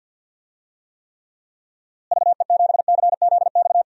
Morse code, heard as a keyed tone, about 51 wpm, 700 Hz. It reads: VE7CZX
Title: DXCC-100 Call Signs - 6 Characters - Repeated 2x - 50wpm